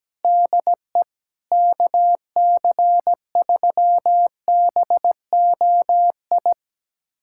Morse code, heard as a keyed tone, about 17 wpm, 700 Hz. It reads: DE KC3BOI
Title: USA Callsign DE Callsign 17wpm